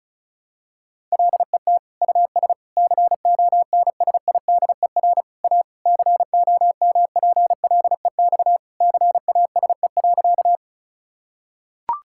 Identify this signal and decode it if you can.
Morse code, heard as a keyed tone, about 36 wpm, 700 Hz. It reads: LET US CONSIDER A COMPLEX CASE.
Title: Sentences from Top 1000 Words - Review Part 6 - 35wpm